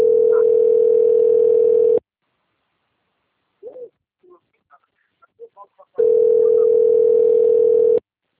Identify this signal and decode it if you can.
Ringback tone (North America), 440 + 480 Hz, with speech.
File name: out- - - - - .30982